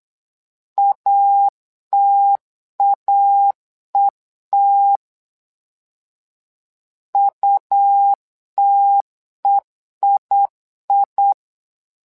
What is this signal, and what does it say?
Morse code, heard as a keyed tone, about 8 wpm, 800 Hz. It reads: ATAET UTEII
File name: LEZIONI / 2 UHDN / 2 UH 8wpm 2129